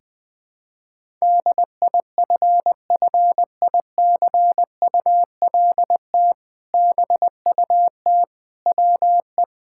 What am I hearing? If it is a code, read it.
Morse code, heard as a keyed tone, about 21 wpm, 700 Hz. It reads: DIFFICULT BUT WE
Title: Sentences with 2-Word Phrases - 20wpm